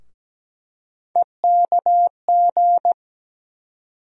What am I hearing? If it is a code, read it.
Morse code, heard as a keyed tone, about 17 wpm, 700 Hz. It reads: EKG
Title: USA Call Sign Postfixes - 3 Letters 17wpm